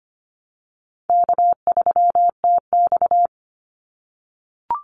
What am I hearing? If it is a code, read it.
Morse code, heard as a keyed tone, about 25 wpm, 700 Hz. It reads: K3TX
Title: Call Signs - 25wpm